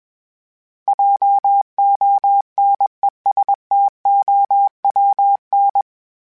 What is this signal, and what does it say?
Morse code, heard as a keyed tone, about 21 wpm, 800 Hz. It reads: JONESTOWN
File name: fromSMs / FromSMs mp3s / Cities@20